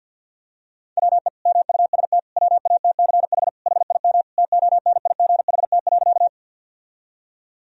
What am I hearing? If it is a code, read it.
Morse code, heard as a keyed tone, about 50 wpm, 700 Hz. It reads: WE MUST WATCH HIM TONIGHT.
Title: Sentences from Top 1000 Words - Review Part 3 - 50wpm